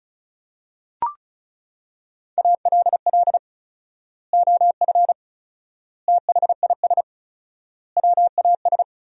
Morse code, heard as a keyed tone, about 36 wpm, 700 Hz. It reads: ALL OF THIS WAS
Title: 4x Word-Spacing - Sentences from Top 500 Words - 35wpm